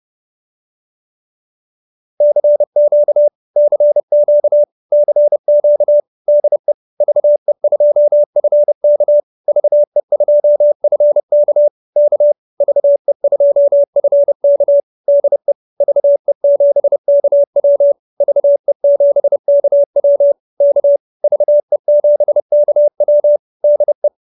Morse code, heard as a keyed tone, about 30 wpm, 590 Hz. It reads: CQ CQ CQ DE VE2FK VE2FK K VE2FK DE VE7KW VE7KW K VE7KW DE